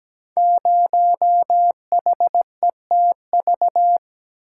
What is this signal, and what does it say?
Morse code, heard as a keyed tone, about 17 wpm, 700 Hz. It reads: 0HETV
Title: Code Groups - 5 Letter:Numbers - 17wpm